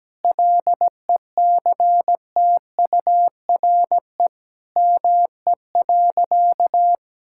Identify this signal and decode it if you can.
Morse code, heard as a keyed tone, about 17 wpm, 700 Hz. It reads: LECTURE ME.
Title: Sentences from Top 1600 Words 17wpm